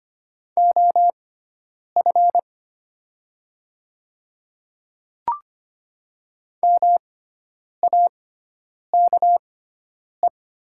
Morse code, heard as a keyed tone, about 25 wpm, spaced as for 9 wpm, 700 Hz. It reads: OF MAKE
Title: 6x Character Spacing - Top 100 Words - 25wpm